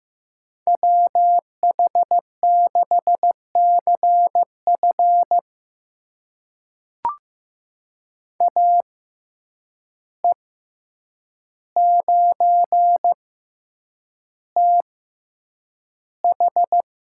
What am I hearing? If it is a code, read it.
Morse code, heard as a keyed tone, about 15 wpm, spaced as for 8 wpm, 700 Hz. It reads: WH6CF AE9TH
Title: USA Call Signs - 5 Characters - 6x Character Spacing - 15wpm